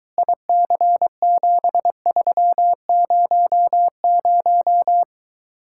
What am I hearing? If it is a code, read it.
Morse code, heard as a keyed tone, about 22 wpm, 700 Hz. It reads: IC7300